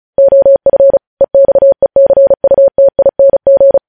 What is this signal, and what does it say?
Morse code, heard as a keyed tone, about 36 wpm, 550 Hz. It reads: OF EXECUTING
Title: News Headlines at 35